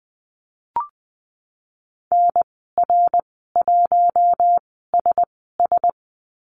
Morse code, heard as a keyed tone, about 20 wpm, spaced as for 14 wpm, 700 Hz. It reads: NR1SS
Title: 055 Lesson 10g - Letter r - Call Signs 20wpm